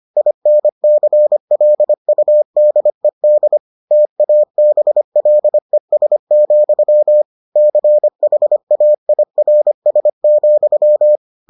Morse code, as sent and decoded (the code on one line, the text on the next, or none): .. -. -.-. .-.. ..- -.. -.. / - .- -... .-.. ... --..-- / -.-. .... .- .. .-. ... --..--
INCLUDED TABLES, CHAIRS,